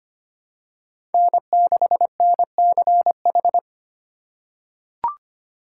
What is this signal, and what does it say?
Morse code, heard as a keyed tone, about 25 wpm, 700 Hz. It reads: N6NCH
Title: USA Call Signs - 5 Characters - 2x Character Spacing - 25wpm